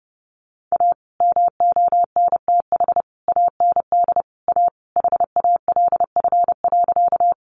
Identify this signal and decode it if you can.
Morse code, heard as a keyed tone, about 30 wpm, 700 Hz. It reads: A MONTH AND A HALF.
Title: Sentences from Top 800 Words 30wpm